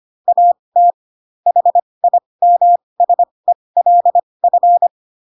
Morse code, heard as a keyed tone, about 25 wpm, spaced as for 20 wpm, 700 Hz. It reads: AT HIMSELF
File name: Pinocchio / Pinocchio ch8 15 25 ik1hge